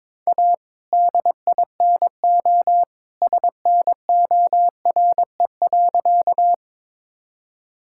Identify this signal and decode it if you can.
Morse code, heard as a keyed tone, about 22 wpm, 700 Hz. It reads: A DINO SNORE.
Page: morse-code-ninja-practice-22wpm